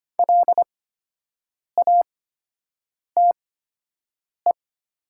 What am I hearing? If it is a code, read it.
Morse code, heard as a keyed tone, about 25 wpm, spaced as for 7 wpm, 700 Hz. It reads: LATE
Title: 8x Character Spacing - Top 300 Words - 25wpm